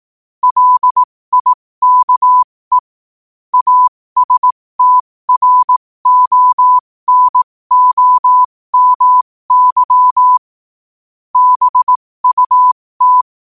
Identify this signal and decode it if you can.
Morse code, heard as a keyed tone, about 18 wpm, spaced as for 15 wpm, 1000 Hz. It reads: LIKE ASTRONOMY BUT
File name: qotdfiles / 15wpm